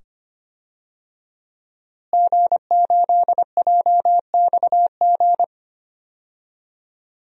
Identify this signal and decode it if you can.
Morse code, heard as a keyed tone, about 25 wpm, 700 Hz. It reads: G8JXG
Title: DXCC-100 Call Signs - 5 Characters - Repeated 2x - 25wpm